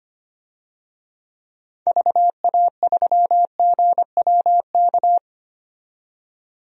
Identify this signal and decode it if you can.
Morse code, heard as a keyed tone, about 25 wpm, 700 Hz. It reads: VA3GWK
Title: DXCC-100 Call Signs - 6 Characters - 25wpm